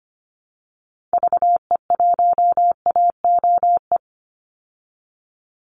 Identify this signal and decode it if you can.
Morse code, heard as a keyed tone, about 25 wpm, 700 Hz. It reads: VE1AOE
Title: Call Signs Repeated 3x - 25wpm